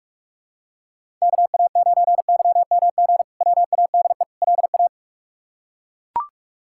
Morse code, heard as a keyed tone, about 41 wpm, 700 Hz. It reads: KA9YMG WADE LA
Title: K1USN SST QSO Practice - 45wpm